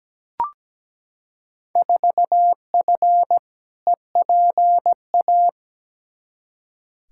Morse code, heard as a keyed tone, about 17 wpm, 700 Hz. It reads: 4F EPA
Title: Field Day Exchanges Encore - 17wpm